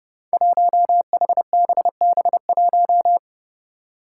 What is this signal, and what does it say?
Morse code, heard as a keyed tone, about 30 wpm, 700 Hz. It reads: 1HBB1
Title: Five Challenging Letter-Number Combinations 30wpm